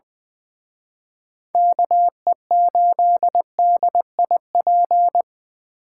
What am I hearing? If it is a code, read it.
Morse code, heard as a keyed tone, about 24 wpm, 700 Hz. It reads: KE8DIP
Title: Speed Racing - USA 2x3 Call Signs 20wpm